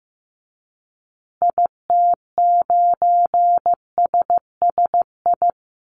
Morse code, heard as a keyed tone, about 15 wpm, 700 Hz. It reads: IT9SSI
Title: Call Signs Repeated 3x - 15wpm